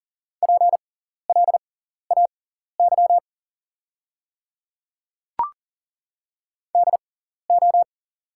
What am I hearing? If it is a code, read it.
Morse code, heard as a keyed tone, about 40 wpm, spaced as for 14 wpm, 700 Hz. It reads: PLAY DO
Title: 6x Character Spacing - Top 200 Words - 40wpm